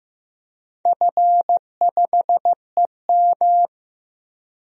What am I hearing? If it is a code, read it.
Morse code, heard as a keyed tone, about 15 wpm, 700 Hz. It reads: F5EM
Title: DXCC-100 Call Signs - 4 Characters - Repeated 3x - 15wpm